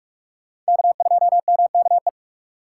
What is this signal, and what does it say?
Morse code, heard as a keyed tone, about 51 wpm, 700 Hz. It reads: K2MKE
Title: Speed Racing - USA 1x3 Call Signs 45wpm